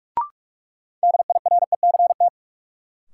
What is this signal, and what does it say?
Morse code, heard as a keyed tone, about 46 wpm, 700 Hz. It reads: DIRECT
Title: Sentences from Top 400 Words - New Word Drill - 45wpm